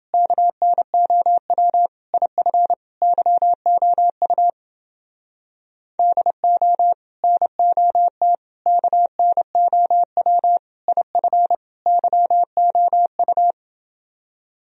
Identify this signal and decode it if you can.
Morse code, heard as a keyed tone, about 29 wpm, 700 Hz. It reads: KNOW IF YOU DO NOT KNOW IF YOU